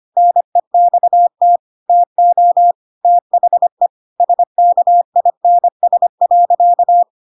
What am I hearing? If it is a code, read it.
Morse code, heard as a keyed tone, about 25 wpm, 700 Hz. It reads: NEXT TO THE SKINS.
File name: Pinocchio / Pinocchio ch7 25 25 ik1hge